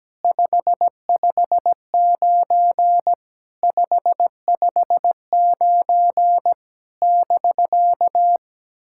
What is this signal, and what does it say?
Morse code, heard as a keyed tone, about 17 wpm, 700 Hz. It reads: 559 559 <BK>
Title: SOTA QSO Practice 17wpm